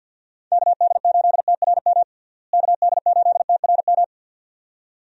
Morse code, heard as a keyed tone, about 47 wpm, 700 Hz. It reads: KD8TRM KD8TRM
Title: USA Call Signs - 6 Characters - Repeated 2x - 50wpm